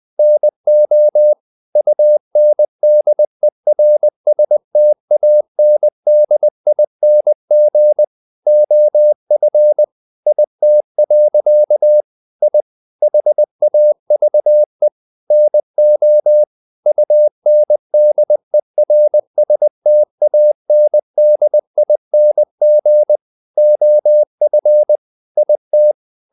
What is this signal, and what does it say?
Morse code, heard as a keyed tone, about 20 wpm, 600 Hz. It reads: NO UNDERSTANDING OF IT. I HAVE NO UNDERSTANDING OF IT
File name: the old man and the sea part 2 20wpm